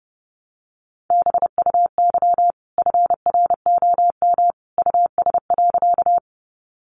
Morse code, heard as a keyed tone, about 30 wpm, 700 Hz. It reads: BUY FROM US.
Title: Sentences from Top 700 Words 30wpm